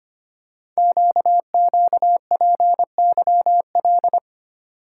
Morse code, heard as a keyed tone, about 25 wpm, 700 Hz. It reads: QQPYL
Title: Code Groups - 5 Letters - 25wpm